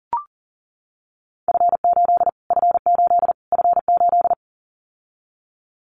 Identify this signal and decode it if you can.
Morse code, heard as a keyed tone, about 40 wpm, 700 Hz. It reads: F8 F8 F8
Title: Callsign Prefixes for Top 25 DXCC - Repeated 3x - 40wpm